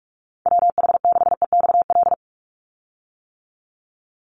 Morse code, heard as a keyed tone, about 44 wpm, 700 Hz. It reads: WH6EXL
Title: USA Call Signs 45wpm